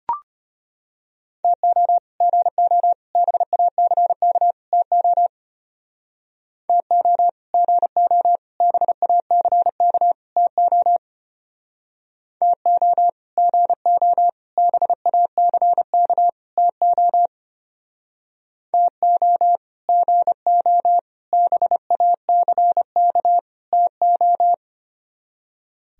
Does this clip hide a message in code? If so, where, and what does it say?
Yes: Morse code from 1.4 to 24.5 s, heard as a keyed tone, about 32 wpm, 700 Hz. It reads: TO GO BACK TO TO GO BACK TO TO GO BACK TO TO GO BACK TO